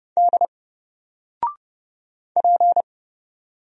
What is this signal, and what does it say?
Morse code, heard as a keyed tone, about 30 wpm, 700 Hz. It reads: D P
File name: Single-Letter-Number-30wpm